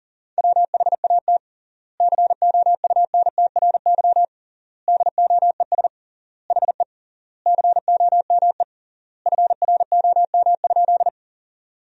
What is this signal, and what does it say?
Morse code, heard as a keyed tone, about 40 wpm, 700 Hz. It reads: WHAT COUNTRY DOES HE COME FROM?
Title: 3x Word-Spacing - Sentences from Top 300 Words - 40wpm